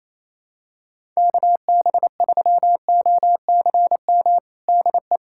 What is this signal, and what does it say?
Morse code, heard as a keyed tone, about 27 wpm, 700 Hz. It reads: KB3OCM DE
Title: USA Callsign DE Callsign 28wpm